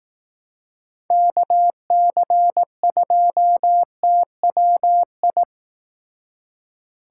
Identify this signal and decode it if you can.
Morse code, heard as a keyed tone, about 18 wpm, 700 Hz. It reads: KC2TWI
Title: Speed Racing - USA Call Signs 15wpm